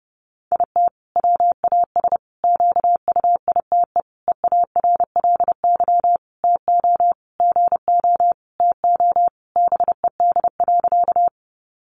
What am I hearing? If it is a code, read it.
Morse code, heard as a keyed tone, about 30 wpm, 700 Hz. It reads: IT WAS QUITE EARLY TO GO TO BED.
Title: Sentences from Top 700 Words 30wpm